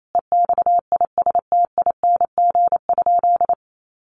Morse code, heard as a keyed tone, about 28 wpm, 700 Hz. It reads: EXISTING?
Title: Sentences from Top 1500 Words 28wpm